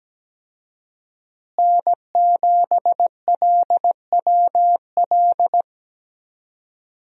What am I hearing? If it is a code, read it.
Morse code, heard as a keyed tone, about 17 wpm, 700 Hz. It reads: N7LWL
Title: USA Call Signs - 5 Characters - Repeated 3x - 17wpm